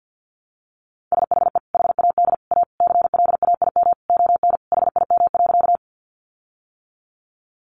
Morse code, heard as a keyed tone, about 50 wpm, 700 Hz. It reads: SHE HAD A CLAIM ON HIM.
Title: Sentences from Top 1500 Words 50wpm